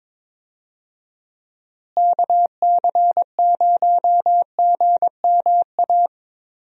Morse code, heard as a keyed tone, about 22 wpm, 700 Hz. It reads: KC0GMA
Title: USA Call Signs - 6 Characters - Repeated 2x - 22wpm